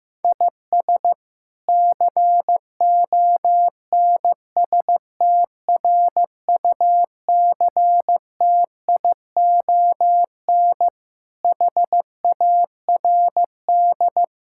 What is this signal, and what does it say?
Morse code, heard as a keyed tone, about 15 wpm, 700 Hz. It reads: IS CONSTRUCTION HARD
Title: Sentences from Top 1600 Words 15wpm